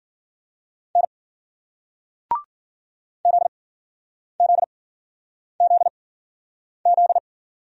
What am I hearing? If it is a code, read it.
Morse code, heard as a keyed tone, about 54 wpm, spaced as for 9 wpm, 700 Hz. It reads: N ZZZZ